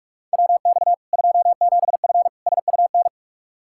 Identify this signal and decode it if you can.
Morse code, heard as a keyed tone, about 45 wpm, 700 Hz. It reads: WX 27F SUN
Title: Practice QSO - Exchange 2 - Info - 45wpm